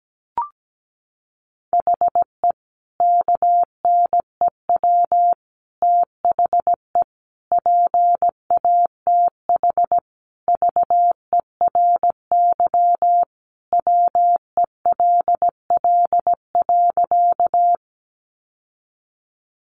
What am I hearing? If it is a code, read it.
Morse code, heard as a keyed tone, about 17 wpm, 700 Hz. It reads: HE KNEW THE PATH VERY WELL.
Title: Sentences from Top 1400 Words 17wpm